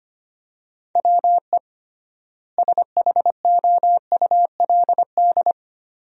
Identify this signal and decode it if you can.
Morse code, heard as a keyed tone, about 25 wpm, 700 Hz. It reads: WE SHOULD
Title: 3x Word-Spacing - Sentences from Top 300 Words - 25wpm